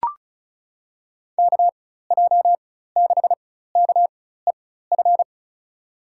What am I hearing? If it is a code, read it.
Morse code, heard as a keyed tone, about 36 wpm, spaced as for 16 wpm, 700 Hz. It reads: KJ6KEF
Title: USA Call Signs - 6 Characters - 4x Character Spacing - 35wpm